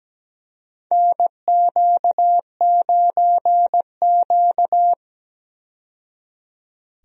Morse code, heard as a keyed tone, about 17 wpm, 700 Hz. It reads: NQ9Q
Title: USA Call Signs - 4 Characters - 17wpm